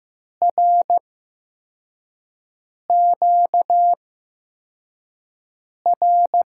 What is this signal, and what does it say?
Morse code, heard as a keyed tone, about 15 wpm, spaced as for 4 wpm, 700 Hz. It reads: RQR
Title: USA Call Signs - 6 Characters - 8x Character Spacing - 15wpm